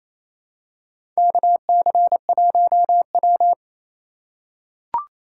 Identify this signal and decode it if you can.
Morse code, heard as a keyed tone, about 28 wpm, 700 Hz. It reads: KC1W
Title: USA Call Signs - 4 Characters - 3x Character Spacing - 28wpm